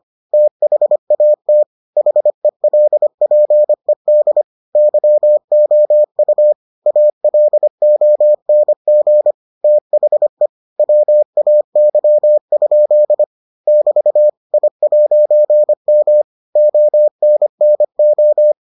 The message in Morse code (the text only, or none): THAT HELPED YOU ALONG THE WAY? <BT> I'M ONNO